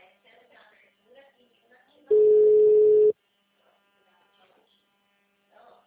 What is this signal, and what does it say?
Ringback tone (Europe, 425 Hz), with speech.